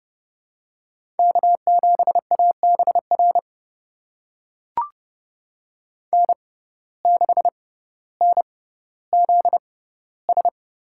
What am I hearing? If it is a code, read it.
Morse code, heard as a keyed tone, about 30 wpm, spaced as for 15 wpm, 700 Hz. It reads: K7ABR N6NZS
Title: USA Call Signs - 5 Characters - 6x Character Spacing - 30wpm